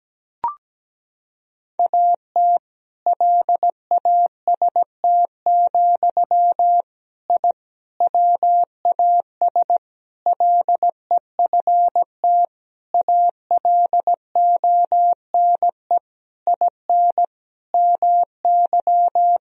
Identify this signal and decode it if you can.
Morse code, heard as a keyed tone, about 17 wpm, 700 Hz. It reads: AT LAST, I WAS LEFT ALONE IN MY
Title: Sentences from Top 1000 Words - Review Part 1 - 17wpm